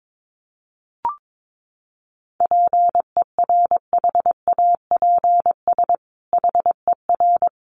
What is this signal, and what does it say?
Morse code, heard as a keyed tone, about 22 wpm, 700 Hz. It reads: PERHAPS HER
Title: Sentences from Top 1500 Words 22wpm